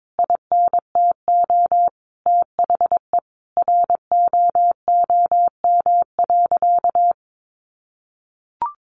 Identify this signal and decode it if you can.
Morse code, heard as a keyed tone, about 22 wpm, 700 Hz. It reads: INTO THE ROOM.
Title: Sentences from Top 600 Words 22wpm